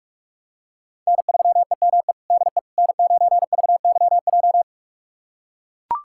Morse code, heard as a keyed tone, about 43 wpm, 700 Hz. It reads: N3EME DE N9VYJ
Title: USA Callsign DE Callsign - 5 Characters - 45wpm